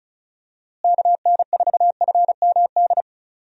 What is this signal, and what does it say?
Morse code, heard as a keyed tone, about 36 wpm, 700 Hz. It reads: KN4FMD
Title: USA Call Signs - 6 Characters - Repeated 3x - 35wpm